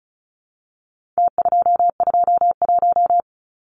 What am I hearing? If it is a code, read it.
Morse code, heard as a keyed tone, about 34 wpm, 700 Hz. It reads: T221
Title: Numbers - Contest Style - 4-Digit 35wpm